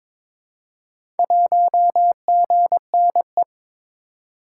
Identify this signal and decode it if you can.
Morse code, heard as a keyed tone, about 22 wpm, 700 Hz. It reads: 1GNE
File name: practice / 22wpm / Four Letter-Number Combinations - 22wpm